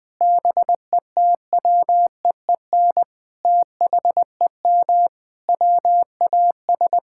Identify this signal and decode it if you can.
Morse code, heard as a keyed tone, about 19 wpm, 700 Hz. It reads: BETWEEN THEM WAS
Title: Sentences from Top 1000 Words - Review Part 6 - 20wpm